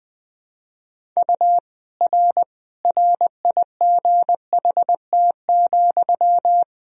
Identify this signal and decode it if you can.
Morse code, heard as a keyed tone, about 20 wpm, 700 Hz. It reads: U R RIGHT,
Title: Sentences from Top 400 Words and Abbreviations 20wpm